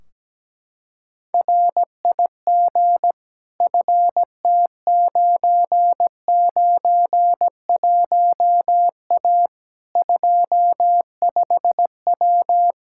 Morse code, heard as a keyed tone, about 17 wpm, 700 Hz. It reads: RIG FT991A 25W
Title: Practice QSO - Exchange 2 - Info - 17wpm